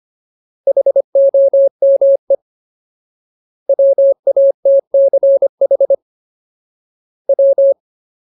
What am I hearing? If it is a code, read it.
Morse code, heard as a keyed tone, about 25 wpm, 550 Hz. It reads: HOME WATCH W
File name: Ultimate Word List / 25WPM Ultimate